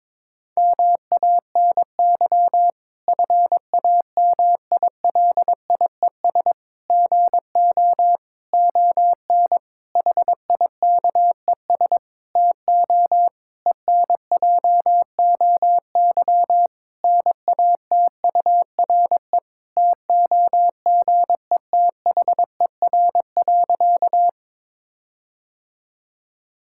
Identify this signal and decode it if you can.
Morse code, heard as a keyed tone, about 22 wpm, 700 Hz. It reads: MANY FAMILIES GO ON HIKES TO ENJOY NATURE TOGETHER.
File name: Difficult Sentences with Hints - 22wpm